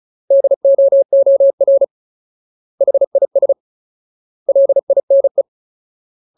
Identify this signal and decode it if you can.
Morse code, heard as a keyed tone, about 34 wpm, 550 Hz. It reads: DOOR HIS LINE
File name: Ultimate Word List / 35WPM Ultimate